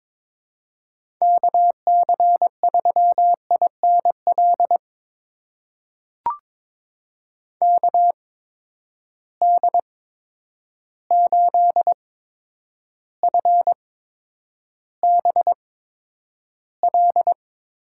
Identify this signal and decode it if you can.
Morse code, heard as a keyed tone, about 22 wpm, spaced as for 9 wpm, 700 Hz. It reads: KC3INL KD8FBL
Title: USA Call Signs - 6 Characters - 8x Character Spacing - 22wpm